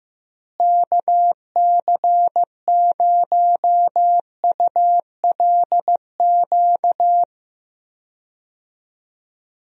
Morse code, heard as a keyed tone, about 16 wpm, 700 Hz. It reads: KC0ULQ
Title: USA Call Signs - 6 Characters - Repeated 3x - 15wpm